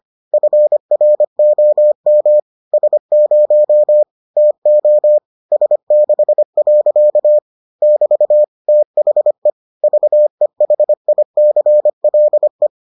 Morse code, heard as a keyed tone, about 25 wpm, 600 Hz. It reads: FROM S0 TO S6. <BT> THE VEHICLE